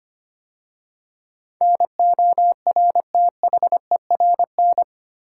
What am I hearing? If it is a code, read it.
Morse code, heard as a keyed tone, about 25 wpm, 700 Hz. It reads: NORTHERN
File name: 8 Letter Words - First 3 Letters Given - 25wpm